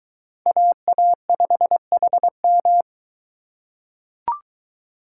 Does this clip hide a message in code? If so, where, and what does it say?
Yes: Morse code from 0.5 to 2.8 s, heard as a keyed tone, about 22 wpm, 700 Hz. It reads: AA5HM